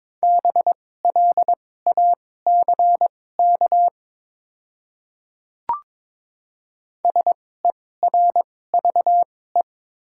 Morse code, heard as a keyed tone, about 22 wpm, spaced as for 16 wpm, 700 Hz. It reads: BLACK SERVE
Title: 2x Character Spacing - Top 500 Words - 22wpm